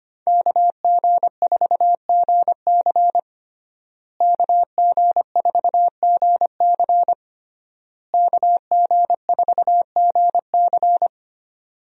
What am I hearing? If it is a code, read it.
Morse code, heard as a keyed tone, about 25 wpm, 700 Hz. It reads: KG4GC KG4GC KG4GC
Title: USA Call Signs - 5 Characters - Repeated 3x - 25wpm